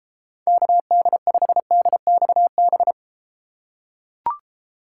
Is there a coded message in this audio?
Yes: Morse code from 0.5 to 2.9 s, heard as a keyed tone, about 31 wpm, 700 Hz. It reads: KD5DXB